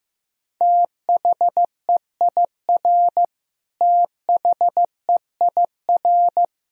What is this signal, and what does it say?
Morse code, heard as a keyed tone, about 15 wpm, 700 Hz. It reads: THEIR THEIR
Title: Top 100 Words Repeated 15wpm